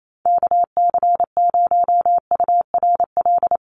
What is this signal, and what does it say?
Morse code, heard as a keyed tone, about 28 wpm, 700 Hz. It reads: KC0URL
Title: Call Signs - 28wpm